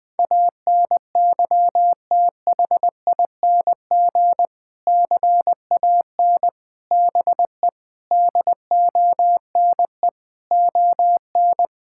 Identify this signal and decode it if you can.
Morse code, heard as a keyed tone, about 20 wpm, 700 Hz. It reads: ANYTHING CAN BE DONE ON
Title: Sentences from Top 1000 Words - Review Part 5 - 20wpm